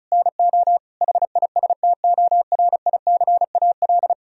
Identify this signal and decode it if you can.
Morse code, heard as a keyed tone, about 36 wpm, 700 Hz. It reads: NO HISTORICAL
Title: Sentences from Top 1000 Words - Review Part 4 - 35wpm